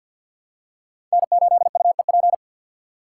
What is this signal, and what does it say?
Morse code, heard as a keyed tone, about 50 wpm, 700 Hz. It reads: N8UEP
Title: Code Groups - 5 Letter:Numbers - 50wpm